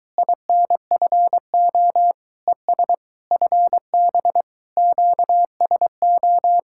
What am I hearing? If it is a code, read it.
Morse code, heard as a keyed tone, about 23 wpm, 700 Hz. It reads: INFO ES FB QSO